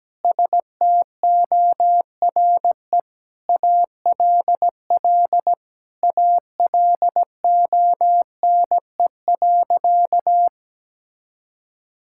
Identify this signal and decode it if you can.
Morse code, heard as a keyed tone, about 17 wpm, 700 Hz. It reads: STORE ALL ALONE.
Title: Sentences from Top 1000 Words - Review Part 5 - 17wpm